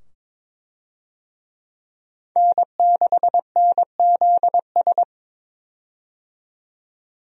Morse code, heard as a keyed tone, about 22 wpm, 700 Hz. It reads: N6NZS
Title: USA Call Signs - 5 Characters - 22wpm